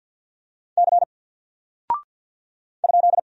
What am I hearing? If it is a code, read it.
Morse code, heard as a keyed tone, about 54 wpm, 700 Hz. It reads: C ?
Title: Single Character (Letter:Number:Punctuation:Prosign) 50wpm